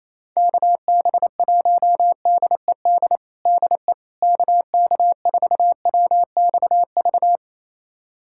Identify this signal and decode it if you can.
Morse code, heard as a keyed tone, about 28 wpm, 700 Hz. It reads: KB1DED DE KK4WXV
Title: USA Callsign DE Callsign - 6 Characters - 28wpm